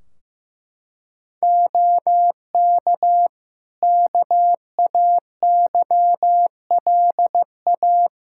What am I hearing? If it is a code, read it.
Morse code, heard as a keyed tone, about 15 wpm, 700 Hz. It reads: OK KAYLA
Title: K1USN SST QSO Practice 15wpm